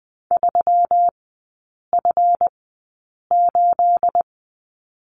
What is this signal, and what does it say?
Morse code, heard as a keyed tone, about 19 wpm, spaced as for 8 wpm, 700 Hz. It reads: 3F8
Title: 212 Lesson 34e - Number 8 - Three Characters 20wpm